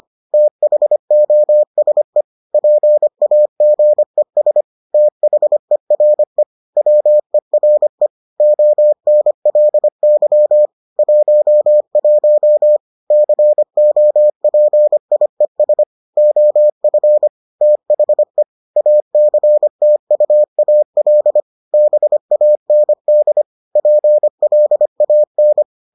Morse code, heard as a keyed tone, about 25 wpm, 600 Hz. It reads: THOSE PAGES THERE WERE ONLY 11 COPIES OF THE ACTUAL BAND PLAN